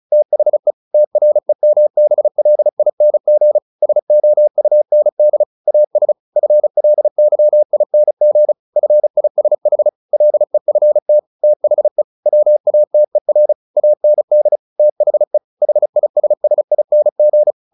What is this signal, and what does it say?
Morse code, heard as a keyed tone, about 35 wpm, 600 Hz. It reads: THE TREMBLING SOUND AS FLYING FISH LEFT THE WATER AND THE HISSING
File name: the old man and the sea / the old man and the sea 1 35wpm